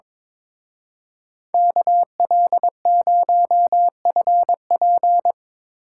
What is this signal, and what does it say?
Morse code, heard as a keyed tone, about 25 wpm, 700 Hz. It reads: KL0FP
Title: Speed Racing - USA 2x2 Call Signs 22wpm